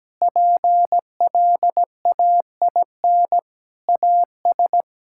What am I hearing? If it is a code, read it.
Morse code, heard as a keyed tone, about 17 wpm, 700 Hz. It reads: PLAIN AS
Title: Sentences from Top 1000 Words - Review Part 5 - 17wpm